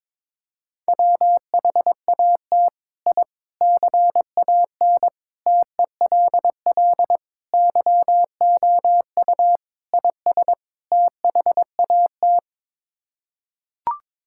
Morse code, heard as a keyed tone, about 22 wpm, 700 Hz. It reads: WHAT I CAN TELL YOU IS THAT
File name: practice / 22wpm / Sets of 7 Words - 22wpm